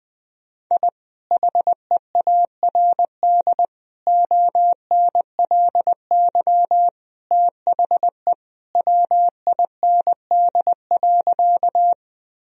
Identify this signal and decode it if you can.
Morse code, heard as a keyed tone, about 20 wpm, 700 Hz. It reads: I HEARD ONLY THE WIND.
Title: Sentences from Top 1000 Words - Review Part 2 - 20wpm